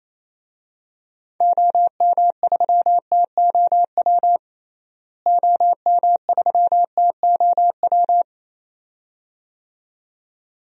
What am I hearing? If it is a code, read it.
Morse code, heard as a keyed tone, about 28 wpm, 700 Hz. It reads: OM3TOW OM3TOW
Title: DXCC-100 Call Signs - 6 Characters - Repeated 2x - 28wpm